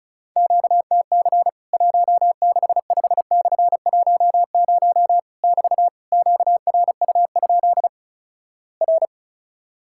Morse code, heard as a keyed tone, about 36 wpm, 700 Hz. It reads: QTC 165/10 <BT> QRU? R
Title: Worked All Europe DX Contest - QTC Exchange - 35wpm